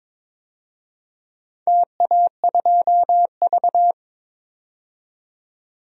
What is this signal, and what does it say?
Morse code, heard as a keyed tone, about 22 wpm, 700 Hz. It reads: TA2V
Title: DXCC-100 Call Signs - 4 Characters - 22wpm